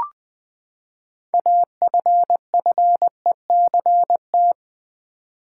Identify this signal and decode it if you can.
Morse code, heard as a keyed tone, about 20 wpm, 700 Hz. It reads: AFFECT
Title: Six Letter Words - 20wpm